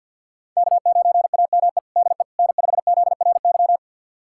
Morse code, heard as a keyed tone, about 53 wpm, 700 Hz. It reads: K9AME DE N5ZRY
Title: USA Callsign DE Callsign - 5 Characters - 50wpm